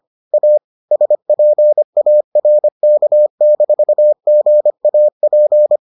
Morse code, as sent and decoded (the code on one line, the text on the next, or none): .- / ... .--. .- .-. -.- -....- --. .- .--.
A SPARK-GAP